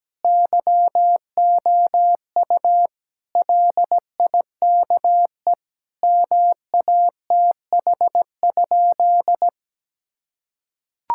Morse code, heard as a keyed tone, about 17 wpm, 700 Hz. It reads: YOU LIKE MATH?
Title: Sentences from Top 1000 Words - Review Part 6 - 17wpm